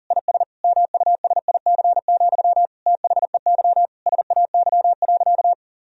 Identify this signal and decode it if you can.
Morse code, heard as a keyed tone, about 40 wpm, 700 Hz. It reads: IS MUSIC, THEY SAY.
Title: Sentences from Top 1000 Words - Review Part 1 - 40wpm